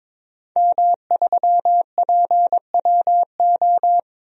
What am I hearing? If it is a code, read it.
Morse code, heard as a keyed tone, about 22 wpm, 700 Hz. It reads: M3PWO
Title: DXCC-100 Call Signs - 5 Characters - 22wpm